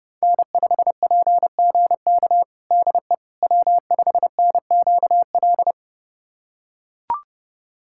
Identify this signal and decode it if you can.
Morse code, heard as a keyed tone, about 30 wpm, 700 Hz. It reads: N5PGK DE W5NQL
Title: USA Callsign DE Callsign - 5 Characters - 30wpm